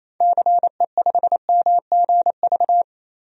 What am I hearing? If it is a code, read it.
Morse code, heard as a keyed tone, about 28 wpm, 700 Hz. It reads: CE5MGV
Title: DXCC-100 Call Signs - 6 Characters - Repeated 3x - 28wpm